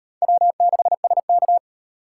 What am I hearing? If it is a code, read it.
Morse code, heard as a keyed tone, about 35 wpm, 700 Hz. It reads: W6SK